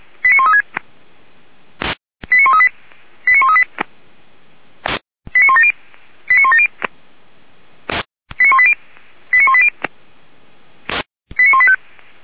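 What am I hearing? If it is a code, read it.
Five-tone selective call (ZVEI), heard as quick runs of five tones, tones of about 80 ms. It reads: 89136 89138 89138 89170 89170 89180 89180 89185
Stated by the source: ZVEI five-tone selective call 89135 received on SDR, from feuerwehr-orlamuende fire brigade server.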